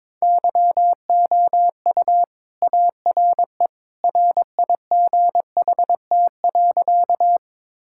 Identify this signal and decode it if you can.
Morse code, heard as a keyed tone, about 22 wpm, 700 Hz. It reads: YOU ARE RIGHT.
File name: practice / 22wpm / Sentences with 2-Word Phrases - 22wpm